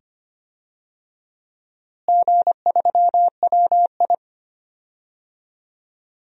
Morse code, heard as a keyed tone, about 25 wpm, 700 Hz. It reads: G3WI
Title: DXCC-100 Call Signs - 4 Characters - Repeated 2x - 25wpm